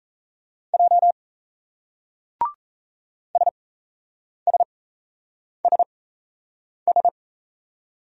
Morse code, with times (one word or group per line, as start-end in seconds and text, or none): 0.7-1.1 J
3.3-7.1 SSSS